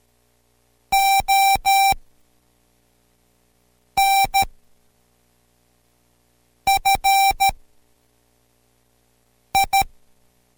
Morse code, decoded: ON FI